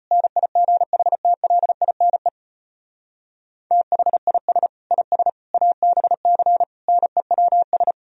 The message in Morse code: NIGHTLINE THIS IS ABC NEWS

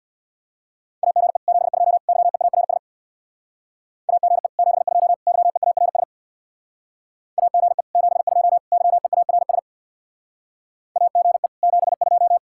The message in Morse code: AGE 72 YEARS AGE 72 YEARS AGE 72 YEARS AGE 72

